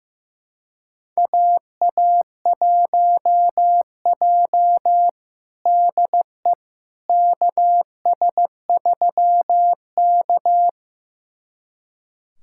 Morse code, heard as a keyed tone, about 15 wpm, 700 Hz. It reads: AA1J DE KS3K
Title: USA Callsign DE Callsign - 4 Characters - 15wpm